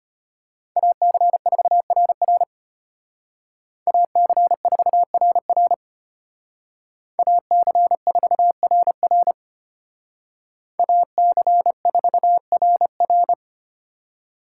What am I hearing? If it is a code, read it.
Morse code, heard as a keyed tone, about 32 wpm, 700 Hz. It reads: AC4RR AC4RR AC4RR AC4RR